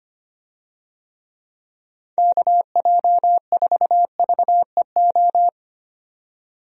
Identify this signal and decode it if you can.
Morse code, heard as a keyed tone, about 25 wpm, 700 Hz. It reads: KJ4VEO
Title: USA Call Signs - 6 Characters - 25wpm